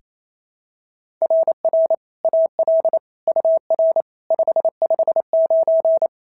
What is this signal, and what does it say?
Morse code, heard as a keyed tone, about 25 wpm, 650 Hz. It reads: RR AL UR 559
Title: SOTA QSO Practice 28wpm